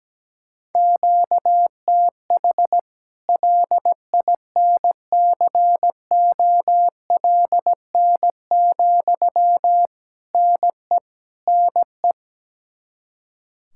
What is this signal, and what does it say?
Morse code, heard as a keyed tone, about 17 wpm, 700 Hz. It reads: QTH LINCOLN, NE NE
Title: Practice QSO - Exchange 1 - Beginning - 17wpm